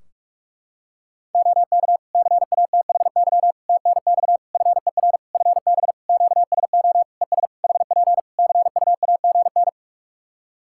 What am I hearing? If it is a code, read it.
Morse code, heard as a keyed tone, about 46 wpm, 700 Hz. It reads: OK CATHY TNX FER FB QSO ES HP CUAGN
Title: K1USN SST QSO Practice 45wpm